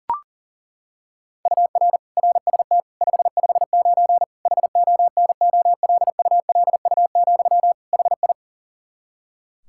Morse code, heard as a keyed tone, about 40 wpm, 700 Hz. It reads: UR RST 559 HONOLULU, HI
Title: Practice QSO - Exchange 1 - Beginning - 40wpm